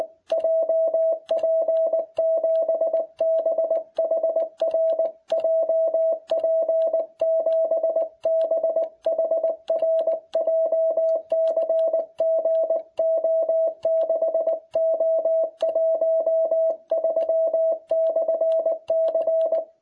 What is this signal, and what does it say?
Morse code, heard as a keyed tone, about 18 wpm, 660 Hz. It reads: JP7BHR JP7BHR JCG O6O13/C